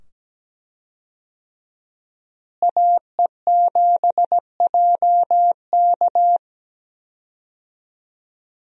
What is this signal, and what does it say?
Morse code, heard as a keyed tone, about 17 wpm, 700 Hz. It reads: AE7JK
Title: USA Call Signs - 5 Characters - Repeated 2x - 17wpm